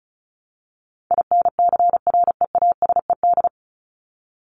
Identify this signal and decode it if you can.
Morse code, heard as a keyed tone, about 34 wpm, 700 Hz. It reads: INCREASED
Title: ED Suffix - Top 100 Words with Base Words 35wpm